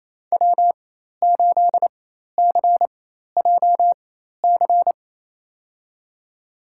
Morse code, heard as a keyed tone, about 28 wpm, spaced as for 13 wpm, 700 Hz. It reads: W8CJC
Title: USA Call Signs - 5 Characters - 4x Character Spacing - 28wpm